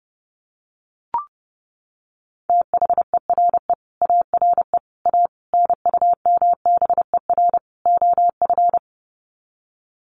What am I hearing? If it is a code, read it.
Morse code, heard as a keyed tone, about 30 wpm, 700 Hz. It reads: THERE ARE A NUMBER OF
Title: Sets of 5 Words 30wpm